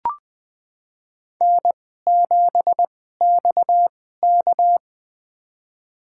Morse code, heard as a keyed tone, about 20 wpm, spaced as for 15 wpm, 700 Hz. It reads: N7XK
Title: USA Call Signs - 2x Character Spacing - 20wpm